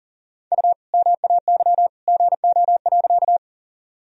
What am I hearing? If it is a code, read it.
Morse code, heard as a keyed tone, about 40 wpm, 700 Hz. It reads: U MAY GO.
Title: Sentences from Top 100 Words and Abbreviations 40wpm